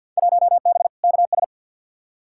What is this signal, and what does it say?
Morse code, heard as a keyed tone, about 47 wpm, 700 Hz. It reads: 1B KS
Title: Field Day Exchanges Encore - 50wpm